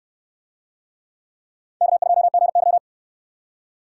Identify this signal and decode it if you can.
Morse code, heard as a keyed tone, about 69 wpm, 700 Hz. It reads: K2GY